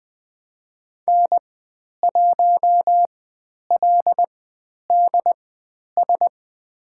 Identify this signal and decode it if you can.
Morse code, heard as a keyed tone, about 20 wpm, spaced as for 10 wpm, 700 Hz. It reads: N1LDS
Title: 102 Lesson 17g - Letter u - Call Signs 20wpm